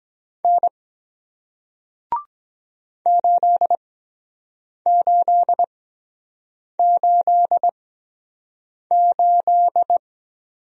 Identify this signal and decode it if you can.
Morse code, heard as a keyed tone, about 23 wpm, spaced as for 7 wpm, 700 Hz. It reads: N 8888